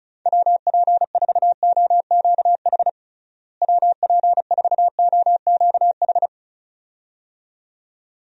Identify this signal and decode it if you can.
Morse code, heard as a keyed tone, about 36 wpm, 700 Hz. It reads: WP4OQH WP4OQH
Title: USA Call Signs - 6 Characters - Repeated 2x - 35wpm